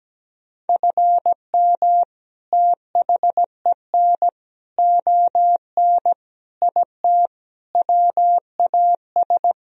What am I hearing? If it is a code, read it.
Morse code, heard as a keyed tone, about 17 wpm, 700 Hz. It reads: FM THEN ON IT WAS
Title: Sentences from Top 400 Words and Abbreviations 17wpm